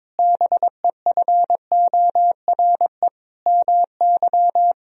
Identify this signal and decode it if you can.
Morse code, heard as a keyed tone, about 21 wpm, 700 Hz. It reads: BEFORE MY
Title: Sentences from Top 1000 Words - Review Part 1 - 22wpm